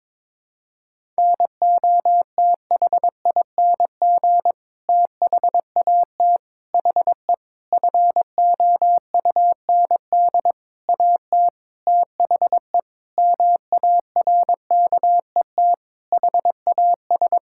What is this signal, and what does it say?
Morse code, heard as a keyed tone, about 21 wpm, 700 Hz. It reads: NOTHING THAT HE FOUND AT THE MARKET HAS
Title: Sentences from Top 1000 Words - Review Part 6 - 22wpm